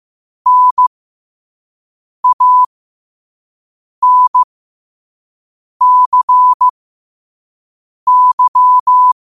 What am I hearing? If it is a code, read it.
Morse code, heard as a keyed tone, about 15 wpm, spaced as for 5 wpm, 1000 Hz. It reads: NANCY